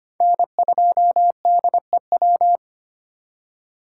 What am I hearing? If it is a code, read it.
Morse code, heard as a keyed tone, about 32 wpm, 700 Hz. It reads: N2DEW